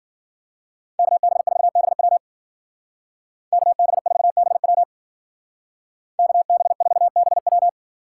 Morse code, heard as a keyed tone, about 52 wpm, 700 Hz. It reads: KB4BW KB4BW KB4BW